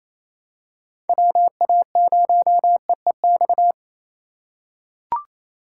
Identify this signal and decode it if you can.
Morse code, heard as a keyed tone, about 28 wpm, 700 Hz. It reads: WA0EEX
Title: USA Call Signs - 6 Characters - 3x Character Spacing - 28wpm